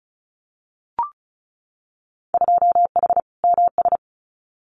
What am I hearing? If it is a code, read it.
Morse code, heard as a keyed tone, about 36 wpm, 700 Hz. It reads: 2H MS
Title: Winter Field Day Exchanges 35wpm